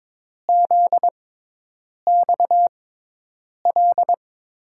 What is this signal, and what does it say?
Morse code, heard as a keyed tone, about 22 wpm, spaced as for 8 wpm, 700 Hz. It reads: ZXL
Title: USA Call Signs - 6 Characters - 6x Character Spacing - 22wpm